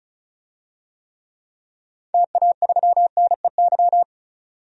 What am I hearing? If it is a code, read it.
Morse code, heard as a keyed tone, about 34 wpm, 700 Hz. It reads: TA3NEY
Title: DXCC-100 Call Signs - 6 Characters - 35wpm